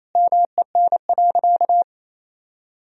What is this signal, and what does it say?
Morse code, heard as a keyed tone, about 28 wpm, 700 Hz. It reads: MEN.
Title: Sentences from Top 1000 Words - Review Part 2 - 28wpm